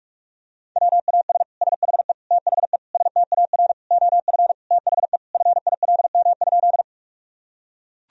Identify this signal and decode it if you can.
Morse code, heard as a keyed tone, about 41 wpm, 700 Hz. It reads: WAS SHE THE STAR OF THE FILM?
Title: Sentences from Top 1000 Words - Review Part 3 - 45wpm